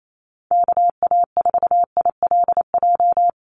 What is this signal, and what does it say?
Morse code, heard as a keyed tone, about 28 wpm, 700 Hz. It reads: KA4ILJ
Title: USA Call Signs 28wpm